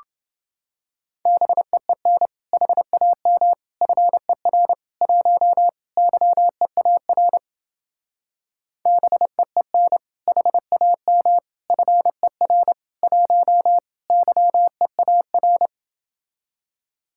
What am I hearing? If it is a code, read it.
Morse code, heard as a keyed tone, about 29 wpm, 700 Hz. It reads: BEEN HAM FER 1 YEAR BEEN HAM FER 1 YEAR